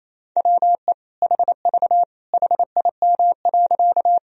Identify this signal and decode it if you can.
Morse code, heard as a keyed tone, about 28 wpm, 700 Hz. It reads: WE HV HIM.
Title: Sentences from Top 100 Words and Abbreviations 28wpm